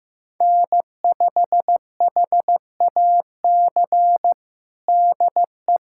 Morse code, heard as a keyed tone, about 15 wpm, 700 Hz. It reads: N5HAC DE
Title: USA Callsign DE Callsign - 5 Characters - 15wpm